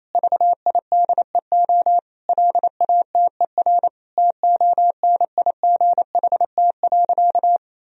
Morse code, heard as a keyed tone, about 28 wpm, 700 Hz. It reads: VIDEO LATER TONIGHT.
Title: Sentences from Top 1000 Words - Review Part 4 - 28wpm